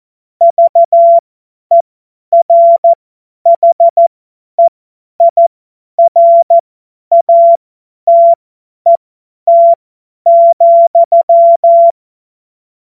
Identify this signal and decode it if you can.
Morse code, heard as a keyed tone, about 14 wpm, spaced as for 10 wpm, 680 Hz. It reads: VERHEIRATET,
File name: Morsekurs / Witze / Witz 05 März 2024 50 F70